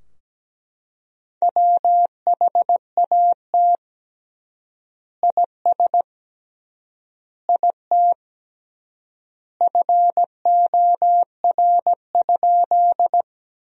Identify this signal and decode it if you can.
Morse code, heard as a keyed tone, about 17 wpm, 700 Hz. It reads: WHAT IS IT FOR?
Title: 3x Word-Spacing - Sentences from Top 100 Words - 17wpm